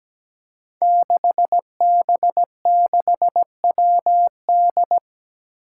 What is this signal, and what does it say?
Morse code, heard as a keyed tone, about 17 wpm, 700 Hz. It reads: 6B6WD
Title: Five Challenging Letter-Number Combinations 17wpm